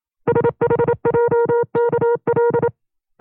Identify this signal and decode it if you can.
Morse code, heard as a keyed tone, about 27 wpm, 470 Hz. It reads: SHJKL
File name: FreeFiles / CW-practice / 25WPM-side1